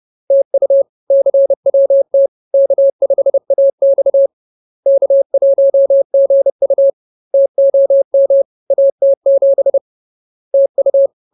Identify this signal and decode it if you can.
Morse code, heard as a keyed tone, about 30 wpm, 550 Hz. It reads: TU CWT K5AX K1GU TOM AT7 TU
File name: CWT-Chapter-06